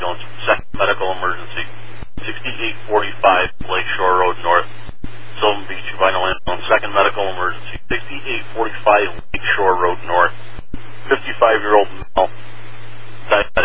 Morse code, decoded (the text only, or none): none